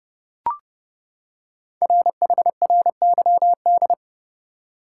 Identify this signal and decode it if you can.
Morse code, heard as a keyed tone, about 30 wpm, 700 Hz. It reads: RHRYD
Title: Code Groups - 5 Letters - 30wpm